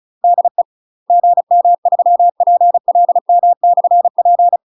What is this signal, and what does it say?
Morse code, heard as a keyed tone, about 36 wpm, 700 Hz. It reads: DE GM3PLM/P
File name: UK MASTER QSOs TRAINING / 35WPM / UK MASTER QSOs 0029